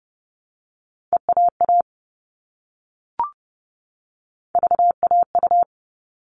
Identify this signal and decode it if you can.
Morse code, heard as a keyed tone, about 30 wpm, 700 Hz. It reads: EAA VAU
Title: All Cut Numbers - 3-Digit 30wpm